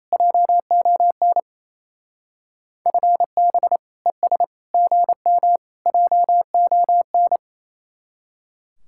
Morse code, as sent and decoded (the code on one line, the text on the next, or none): .--- --- -. / ..-. -... / . ... / --. -- / .--- --- -.
JON FB ES GM JON